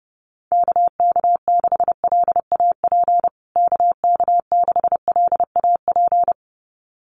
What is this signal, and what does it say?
Morse code, heard as a keyed tone, about 30 wpm, 700 Hz. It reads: KK6LAP KK6LAP
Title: USA Call Signs Repeated 2x 30wpm